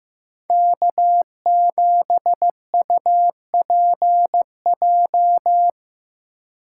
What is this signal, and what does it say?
Morse code, heard as a keyed tone, about 15 wpm, 700 Hz. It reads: K7UPJ
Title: K1USN SST QSO Practice - 15wpm